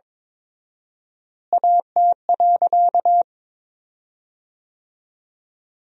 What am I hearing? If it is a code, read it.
Morse code, heard as a keyed tone, about 22 wpm, 700 Hz. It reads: AT.
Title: 4x Word-Spacing - Sentences from Top 200 Words - 22wpm